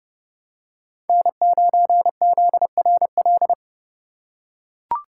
Morse code, heard as a keyed tone, about 32 wpm, 700 Hz. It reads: N9ZRL
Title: USA Call Signs - 3x Character Spacing - 30wpm